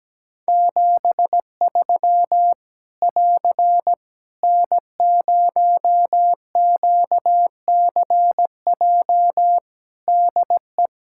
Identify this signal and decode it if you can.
Morse code, heard as a keyed tone, about 17 wpm, 700 Hz. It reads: 73 <AR> N0QCJ DE
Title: K1USN SST QSO Practice 17wpm